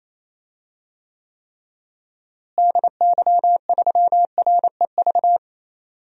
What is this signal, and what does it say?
Morse code, heard as a keyed tone, about 28 wpm, 700 Hz. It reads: DY3REV
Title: DXCC-100 Call Signs - 6 Characters - Repeated 3x - 28wpm